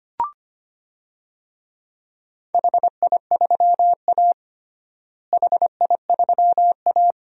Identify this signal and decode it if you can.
Morse code, heard as a keyed tone, about 25 wpm, 700 Hz. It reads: HI3A HI3A
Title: DXCC-100 Call Signs - 4 Characters - Repeated 2x - 25wpm